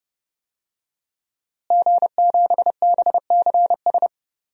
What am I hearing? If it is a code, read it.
Morse code, heard as a keyed tone, about 30 wpm, 700 Hz. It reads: G7BCS
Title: DXCC-100 Call Signs - 5 Characters - Repeated 3x - 30wpm